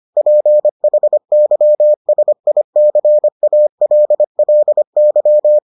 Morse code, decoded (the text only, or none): PHYSICALLY